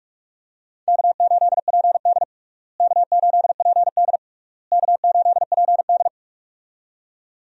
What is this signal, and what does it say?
Morse code, heard as a keyed tone, about 42 wpm, 700 Hz. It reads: K8PD K8PD K8PD
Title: USA Call Signs - 4 Characters - Repeated 3x - 45wpm